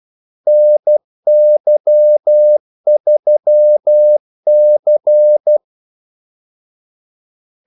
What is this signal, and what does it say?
Morse code, heard as a keyed tone, about 12 wpm, 600 Hz. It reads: NY3C
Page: Kaldesignaler 12 wpm